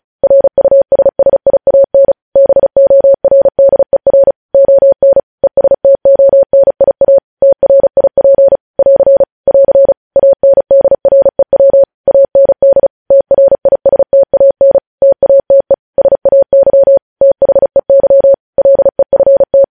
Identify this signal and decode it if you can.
Morse code, heard as a keyed tone, about 36 wpm, 550 Hz. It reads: RUSSIAN BORDER ON ESTONIA TRIP <AR> <AR> ANDREW AND TRISTAN TATE SAY THEY LEFT
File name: podcast / News headlines-1x-35x35